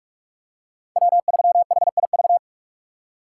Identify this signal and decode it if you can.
Morse code, heard as a keyed tone, about 46 wpm, 700 Hz. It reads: W3HIV
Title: USA Call Signs - 5 Characters - 4x Character Spacing - 45wpm